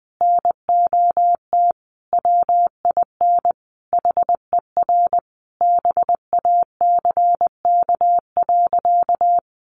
Morse code, heard as a keyed tone, about 20 wpm, 700 Hz. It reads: NOT WIN HER BACK.
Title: Sentences from Top 1500 Words 20wpm